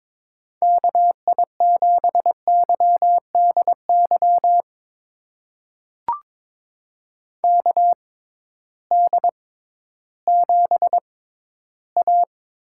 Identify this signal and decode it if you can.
Morse code, heard as a keyed tone, about 22 wpm, spaced as for 13 wpm, 700 Hz. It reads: KI7YDY KD7A
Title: USA Call Signs - 6 Characters - 6x Character Spacing - 22wpm